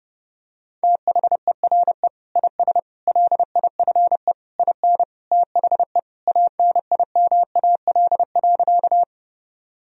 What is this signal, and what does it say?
Morse code, heard as a keyed tone, about 30 wpm, 700 Hz. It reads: THERE IS LIFE IN THE ANIMAL.
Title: Sentences from Top 1000 Words - Review Part 1 - 30wpm